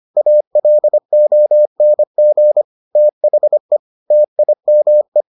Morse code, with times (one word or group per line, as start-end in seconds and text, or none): 0.2-2.6 ALONG
3.0-3.8 THE
4.1-5.2 TIME